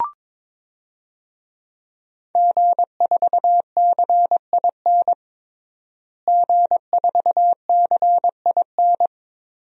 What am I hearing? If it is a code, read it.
Morse code, heard as a keyed tone, about 22 wpm, 700 Hz. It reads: G4CIN G4CIN
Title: DXCC-100 Call Signs - 5 Characters - Repeated 2x - 22wpm